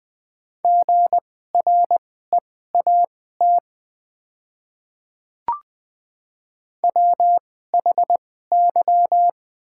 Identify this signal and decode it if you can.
Morse code, heard as a keyed tone, about 20 wpm, spaced as for 14 wpm, 700 Hz. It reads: GREAT WHY
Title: 2x Character Spacing - Top 200 Words - 20wpm